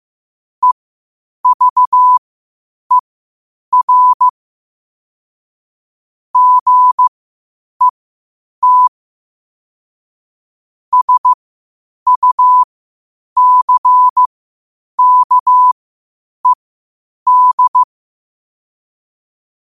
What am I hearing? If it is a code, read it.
Morse code, heard as a keyed tone, about 15 wpm, spaced as for 8 wpm, 1000 Hz. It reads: EVER GET SUCKED